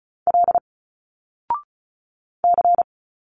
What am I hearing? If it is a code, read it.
Morse code, heard as a keyed tone, about 35 wpm, 700 Hz. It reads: L C
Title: Single Letters 35wpm